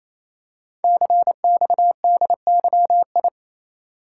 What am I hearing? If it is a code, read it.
Morse code, heard as a keyed tone, about 27 wpm, 700 Hz. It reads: CXDYI
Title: Code Groups - 5 Letters - 28wpm